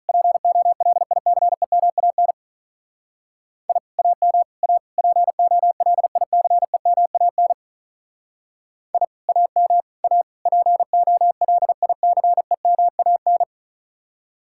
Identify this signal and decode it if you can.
Morse code, heard as a keyed tone, about 45 wpm, 700 Hz. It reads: POLICEMAN I AM A POLICEMAN I AM A POLICEMAN